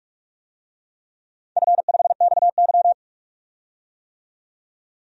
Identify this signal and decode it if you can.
Morse code, heard as a keyed tone, about 44 wpm, 700 Hz. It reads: F5XY
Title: DXCC-100 Call Signs - 4 Characters - 45wpm